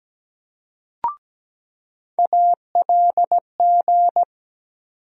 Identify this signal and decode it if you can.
Morse code, heard as a keyed tone, about 17 wpm, 700 Hz. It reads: ALG
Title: USA Call Sign Postfixes - 3 Letters 17wpm